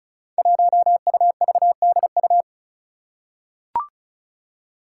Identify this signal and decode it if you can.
Morse code, heard as a keyed tone, about 36 wpm, 700 Hz. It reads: 1UVDU
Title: Five Challenging Letter-Number Combinations 35wpm